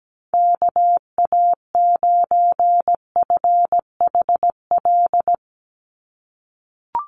Morse code, heard as a keyed tone, about 17 wpm, 700 Hz. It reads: KA9FHL
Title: USA Call Signs 17wpm